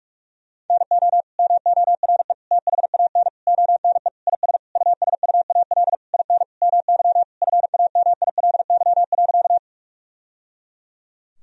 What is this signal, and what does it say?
Morse code, heard as a keyed tone, about 42 wpm, 700 Hz. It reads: NO MORE THAN ONE IS USUAL IN MY FAMILY.
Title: Sentences from Top 1000 Words - Review Part 2 - 45wpm